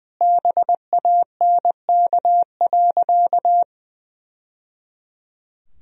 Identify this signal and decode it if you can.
Morse code, heard as a keyed tone, about 21 wpm, 700 Hz. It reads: BANK.
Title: Sentences from Top 1000 Words - Review Part 4 - 20wpm